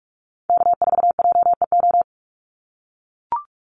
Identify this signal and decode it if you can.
Morse code, heard as a keyed tone, about 44 wpm, 700 Hz. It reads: K4JEO
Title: USA Call Signs 45wpm